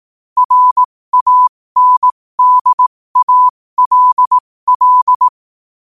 Morse code, heard as a keyed tone, about 18 wpm, spaced as for 15 wpm, 1000 Hz. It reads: RANDALL
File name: qotdfiles / 15wpm